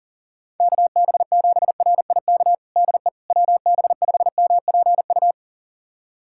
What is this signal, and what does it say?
Morse code, heard as a keyed tone, about 40 wpm, 700 Hz. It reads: KB7RIK DE WB5MPU
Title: USA Callsign DE Callsign - 6 Characters - 40wpm